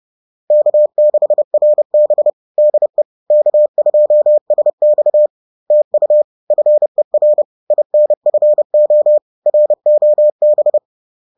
Morse code, heard as a keyed tone, about 30 wpm, 600 Hz. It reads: K6RB DE K2SX TU FER INFO ROB
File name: 30wpm